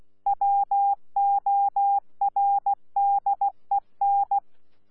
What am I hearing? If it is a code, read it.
Morse code, heard as a keyed tone, about 16 wpm, 800 Hz. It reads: WORDEN
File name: Z PI4VRZ / PI4VRZ 14-jun-2025